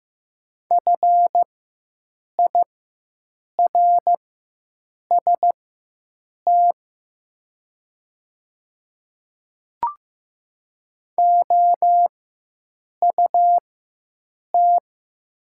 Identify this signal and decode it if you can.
Morse code, heard as a keyed tone, about 16 wpm, spaced as for 7 wpm, 700 Hz. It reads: FIRST OUT
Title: 4x Character Spacing - Top 100 Words - 15wpm